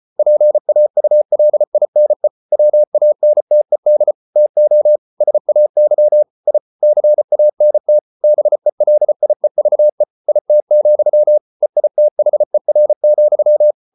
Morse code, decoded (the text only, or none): PAULINE WANTED TO SAY I CANT BELIEVE IT, EITHER,